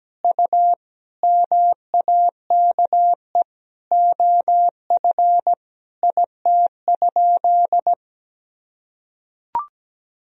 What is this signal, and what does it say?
Morse code, heard as a keyed tone, about 17 wpm, 700 Hz. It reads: U MAKE OF IT?
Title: Sentences from Top 100 Words and Abbreviations 17wpm